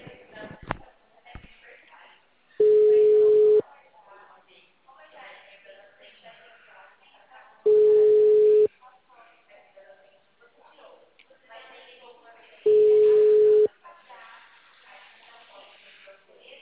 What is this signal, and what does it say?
Ringback tone (Europe, 425 Hz), with speech.
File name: out- -2044- - - .1233